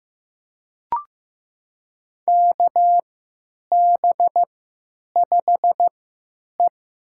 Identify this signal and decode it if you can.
Morse code, heard as a keyed tone, about 15 wpm, spaced as for 9 wpm, 700 Hz. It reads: KB5E
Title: USA Call Signs - 3x Character Spacing - 15wpm